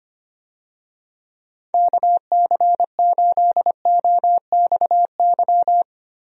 Morse code, heard as a keyed tone, about 25 wpm, 700 Hz. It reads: KC8OXY
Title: USA Call Signs - 6 Characters - Repeated 3x - 25wpm